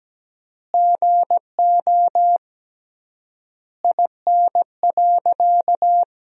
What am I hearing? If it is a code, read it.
Morse code, heard as a keyed tone, about 17 wpm, 700 Hz. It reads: GO IN.
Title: 3x Word-Spacing - Sentences from Top 400 Words - 17wpm